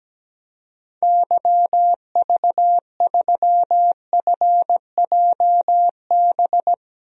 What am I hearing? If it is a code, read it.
Morse code, heard as a keyed tone, about 17 wpm, 700 Hz. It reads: YV3FJB
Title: DXCC-100 Call Signs - 6 Characters - 17wpm